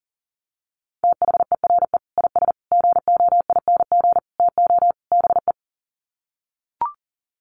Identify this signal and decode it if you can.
Morse code, heard as a keyed tone, about 40 wpm, 700 Hz. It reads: THERE IS GOING TO BE
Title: Sets of 5 Words Encore 40wpm